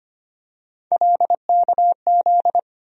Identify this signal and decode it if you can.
Morse code, heard as a keyed tone, about 25 wpm, 700 Hz. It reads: LKZ
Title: USA Call Sign Postfixes - 3 Letters 25wpm